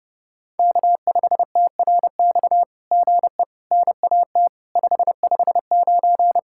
Morse code, heard as a keyed tone, about 28 wpm, 700 Hz. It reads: K5TRX GE NAT 559
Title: SOTA QSO Practice 30wpm